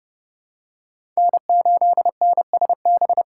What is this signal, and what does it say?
Morse code, heard as a keyed tone, about 30 wpm, 700 Hz. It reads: N8NSB
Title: USA Call Signs - 5 Characters - 3x Character Spacing - 30wpm